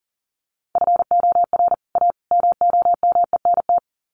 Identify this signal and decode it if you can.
Morse code, heard as a keyed tone, about 40 wpm, 700 Hz. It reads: FOR A MOMENT
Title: Sets of 3 Words 40wpm